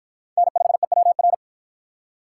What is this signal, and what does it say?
Morse code, heard as a keyed tone, about 50 wpm, 700 Hz. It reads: N5EWR